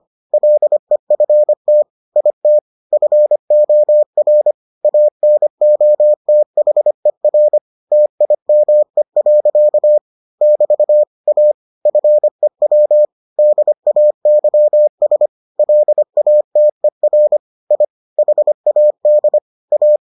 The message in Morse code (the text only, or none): LEFT IT FOR ANOTHER TIME. <BT> A FEW DAYS LATER I HAD A